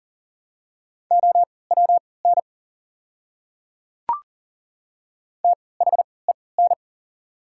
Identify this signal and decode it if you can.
Morse code, heard as a keyed tone, about 39 wpm, spaced as for 22 wpm, 700 Hz. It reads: OWN THEN
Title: 3x Character Spacing - Top 300 Words - 40wpm